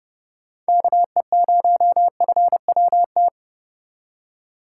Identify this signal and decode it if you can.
Morse code, heard as a keyed tone, about 31 wpm, 700 Hz. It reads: KE0FWT